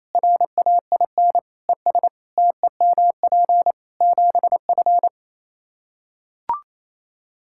Morse code, heard as a keyed tone, about 28 wpm, 700 Hz. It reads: RAIN ES TEMP 7F
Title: Practice QSO - Exchange 2 - Info - 28wpm